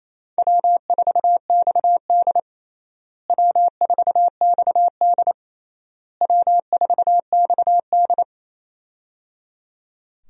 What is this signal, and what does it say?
Morse code, heard as a keyed tone, about 28 wpm, 700 Hz. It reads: W4XD W4XD W4XD
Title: USA Call Signs - 4 Characters - Repeated 3x - 28wpm